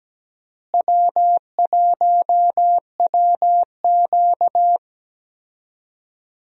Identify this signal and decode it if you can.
Morse code, heard as a keyed tone, about 17 wpm, 700 Hz. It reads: W1WQ
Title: USA Call Signs - 4 Characters - Repeated 2x - 17wpm